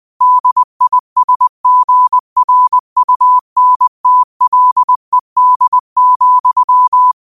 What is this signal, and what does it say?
Morse code, heard as a keyed tone, about 20 wpm, 1000 Hz. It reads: DISGRUNTLED,
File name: qotdfiles / 20wpm